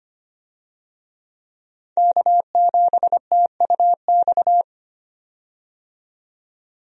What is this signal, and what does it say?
Morse code, heard as a keyed tone, about 25 wpm, 700 Hz. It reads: K7TUX
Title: USA Call Signs - 5 Characters - 25wpm